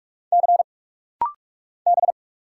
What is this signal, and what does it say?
Morse code, heard as a keyed tone, about 44 wpm, 690 Hz. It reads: C B
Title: Single Letter-Number - Rapid-Fire 45wpm